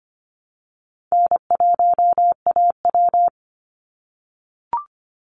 Morse code, heard as a keyed tone, about 25 wpm, 700 Hz. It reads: N1AW
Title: NA SOTA Chaser Callsigns 25wpm